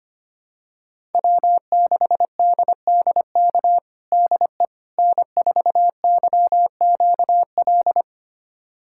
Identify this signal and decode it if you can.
Morse code, heard as a keyed tone, about 25 wpm, 700 Hz. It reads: W6DDK DE N4YQL
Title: USA Callsign DE Callsign - 5 Characters - 25wpm